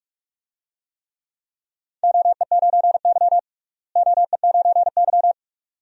Morse code, heard as a keyed tone, about 46 wpm, 700 Hz. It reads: OE9Y OE9Y
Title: DXCC-100 Call Signs - 4 Characters - Repeated 2x - 45wpm